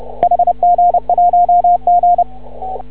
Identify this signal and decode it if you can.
Morse code, heard as a keyed tone, about 30 wpm, 700 Hz, with radio noise. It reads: HG1G
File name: CW / ha6oi 2